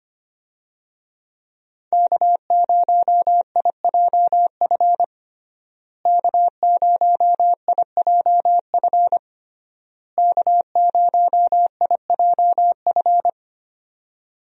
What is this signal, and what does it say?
Morse code, heard as a keyed tone, about 25 wpm, 700 Hz. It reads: K0IJF K0IJF K0IJF
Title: USA Call Signs - 5 Characters - Repeated 3x - 25wpm